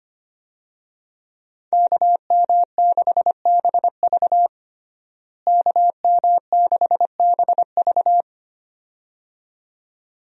Morse code, heard as a keyed tone, about 25 wpm, 700 Hz. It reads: KM6BV KM6BV
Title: USA Call Signs - 5 Characters - Repeated 2x - 25wpm